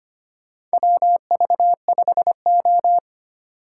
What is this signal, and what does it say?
Morse code, heard as a keyed tone, about 25 wpm, 700 Hz. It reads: WV5O
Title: USA Call Signs - 4 Characters - 2x Character Spacing - 25wpm